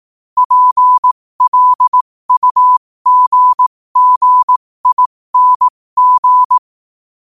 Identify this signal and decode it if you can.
Morse code, heard as a keyed tone, about 18 wpm, spaced as for 15 wpm, 1000 Hz. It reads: PLUGGING
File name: qotdfiles / 15wpm